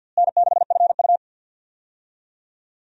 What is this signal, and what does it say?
Morse code, heard as a keyed tone, about 52 wpm, 700 Hz. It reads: N6FU
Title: USA Call Signs - 4 Characters - Repeated 3x - 50wpm